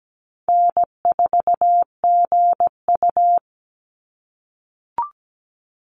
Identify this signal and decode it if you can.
Morse code, heard as a keyed tone, about 17 wpm, 700 Hz. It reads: N4GU
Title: Call Signs - 17wpm